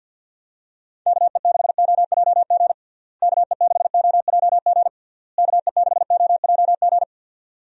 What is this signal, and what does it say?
Morse code, heard as a keyed tone, about 51 wpm, 700 Hz. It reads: KE6OJG KE6OJG KE6OJG
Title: USA Call Signs - 6 Characters - Repeated 3x - 50wpm